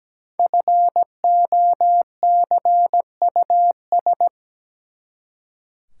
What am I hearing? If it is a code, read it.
Morse code, heard as a keyed tone, about 17 wpm, 700 Hz. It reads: FOCUS
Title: Five Letter Words - 17wpm